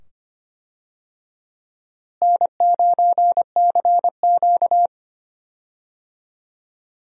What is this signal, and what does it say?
Morse code, heard as a keyed tone, about 25 wpm, 700 Hz. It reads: N9CQ
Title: USA Call Signs - 4 Characters - Repeated 2x - 25wpm